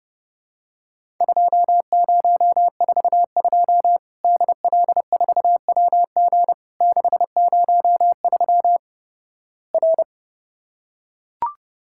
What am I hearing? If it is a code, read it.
Morse code, heard as a keyed tone, about 32 wpm, 700 Hz. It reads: 2042 DL4WG 603 R
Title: Worked All Europe DX Contest - QTC Exchange - 30wpm